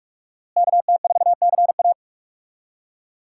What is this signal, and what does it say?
Morse code, heard as a keyed tone, about 43 wpm, 700 Hz. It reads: KT4CA
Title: USA Call Signs - 5 Characters - Repeated 2x - 45wpm